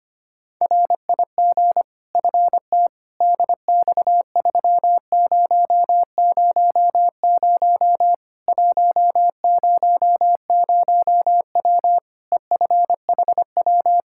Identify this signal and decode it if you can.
Morse code, heard as a keyed tone, about 25 wpm, 700 Hz. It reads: RIG FT DX3000 100W EFHW
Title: Practice QSO - Exchange 2 - Info - 25wpm